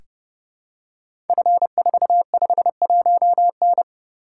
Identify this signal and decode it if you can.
Morse code, heard as a keyed tone, about 28 wpm, 700 Hz. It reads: F451N
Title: Code Groups - 5 Letter:Numbers - 30wpm